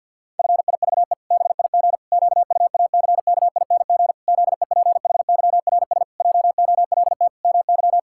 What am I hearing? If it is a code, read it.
Morse code, heard as a keyed tone, about 52 wpm, 700 Hz. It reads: FIVE BIG QUACKING ZEPHYRS JOLT MY
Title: Pangram Challenge v2 - 50wpm